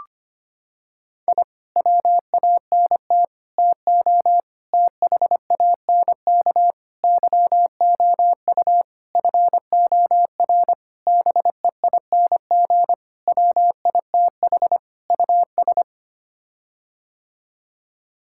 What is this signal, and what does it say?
Morse code, heard as a keyed tone, about 25 wpm, 700 Hz. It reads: I WANT TO THANK YOU FOR BEING WITH US
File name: Sets of 9 Words - 25wpm